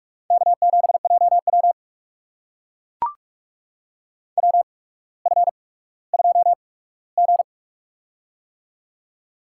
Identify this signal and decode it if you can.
Morse code, heard as a keyed tone, about 44 wpm, spaced as for 19 wpm, 690 Hz. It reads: K7JW WF2G
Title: USA Call Signs - 4 Characters - 8x Character Spacing - 45wpm